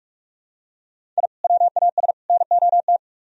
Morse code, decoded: I WAS NOT

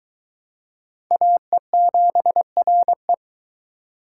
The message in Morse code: AE7RE

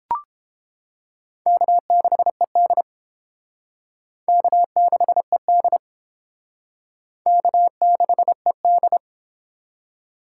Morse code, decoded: K6ED K6ED K6ED

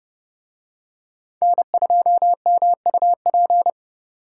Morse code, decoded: N2MUP

